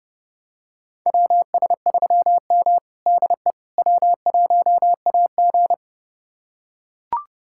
Morse code, heard as a keyed tone, about 30 wpm, 700 Hz. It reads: WS3M DE W1AG
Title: USA Callsign DE Callsign - 4 Characters - 30wpm